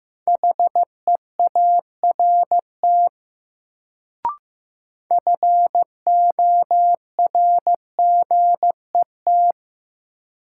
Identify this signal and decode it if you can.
Morse code, heard as a keyed tone, about 15 wpm, 700 Hz. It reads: HEART FORGET
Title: Sentences from Top 1200 Words - New Word Drill Alt - 15wpm